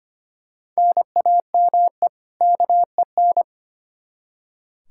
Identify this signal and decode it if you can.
Morse code, heard as a keyed tone, about 25 wpm, 700 Hz. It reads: NAME KEN
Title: Practice QSO - Exchange 1 - Beginning - 25wpm